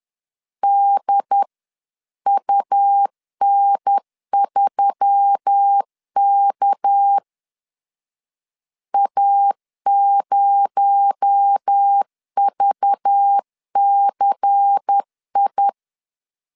Morse code, heard as a keyed tone, about 10 wpm, 800 Hz. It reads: DUN3K A0VCI
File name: CW / DET / 1 6 1500C S B 10wpm 4137